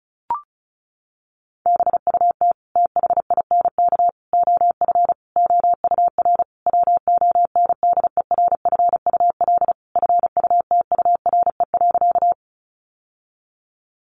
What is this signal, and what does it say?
Morse code, heard as a keyed tone, about 34 wpm, 700 Hz. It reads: BUT THINK OF OUR WONDERFUL FUTURE.
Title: Sentences from Top 1100 Words 35wpm